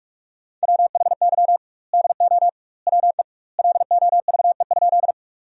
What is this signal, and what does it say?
Morse code, heard as a keyed tone, about 48 wpm, 700 Hz. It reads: WHY DO WE LOVE?
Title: 2x Word-Spacing - Sentences from Top 400 Words - 45wpm